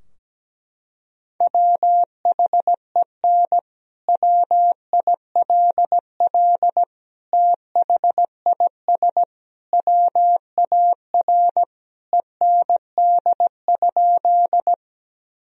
Morse code, heard as a keyed tone, about 17 wpm, 700 Hz. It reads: WHEN WILL THIS WAR END?
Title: Sentences from Top 1000 Words - Review Part 2 - 17wpm